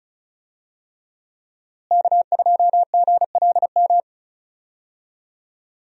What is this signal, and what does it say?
Morse code, heard as a keyed tone, about 34 wpm, 700 Hz. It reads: K2GLM
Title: USA Call Signs - 5 Characters - Repeated 3x - 35wpm